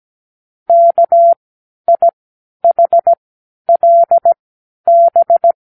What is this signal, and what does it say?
Morse code, heard as a keyed tone, about 17 wpm, spaced as for 10 wpm, 680 Hz. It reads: KIHLB
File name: k7qo / Lesson18(A-L)